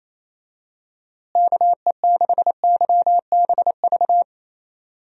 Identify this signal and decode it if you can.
Morse code, heard as a keyed tone, about 27 wpm, 700 Hz. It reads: KE6YBV
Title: USA Call Signs - 6 Characters - Repeated 3x - 28wpm